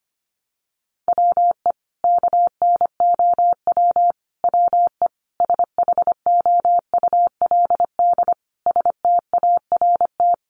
Morse code, heard as a keyed tone, about 25 wpm, 700 Hz. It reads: WE KNOW WE SHOULD START
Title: Sentences from Top 1100 Words 25wpm